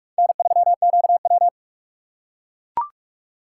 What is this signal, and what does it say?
Morse code, heard as a keyed tone, about 43 wpm, 700 Hz. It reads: N3QW